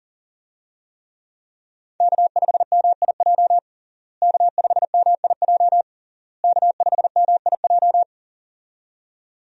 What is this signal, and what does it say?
Morse code, heard as a keyed tone, about 40 wpm, 700 Hz. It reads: K5MIJ K5MIJ K5MIJ
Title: USA Call Signs - 5 Characters - Repeated 3x - 40wpm